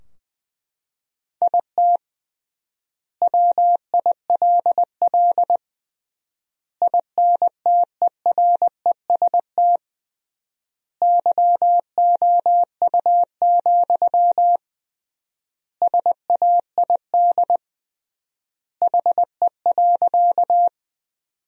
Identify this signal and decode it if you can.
Morse code, heard as a keyed tone, about 21 wpm, 700 Hz. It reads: IT WILL INTEREST YOU, SAID HE.
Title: 3x Word-Spacing - Sentences from Top 400 Words - 20wpm